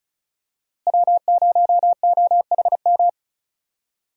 Morse code, heard as a keyed tone, about 34 wpm, 700 Hz. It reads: W0OHM
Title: USA Call Signs - 5 Characters - Repeated 2x - 35wpm